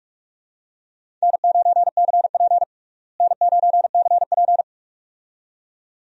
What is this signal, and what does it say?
Morse code, heard as a keyed tone, about 42 wpm, 700 Hz. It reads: N9CP N9CP
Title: USA Call Signs - 4 Characters - Repeated 2x - 45wpm